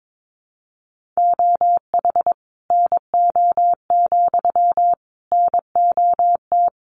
Morse code, heard as a keyed tone, about 22 wpm, 700 Hz. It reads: OH NO, NOT
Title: Sentences from Top 1300 Words 22wpm